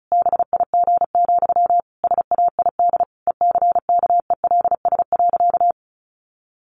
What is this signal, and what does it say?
Morse code, heard as a keyed tone, about 34 wpm, 700 Hz. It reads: BIG, SAID ECKELS.
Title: A Sound Of Thunder 35wpm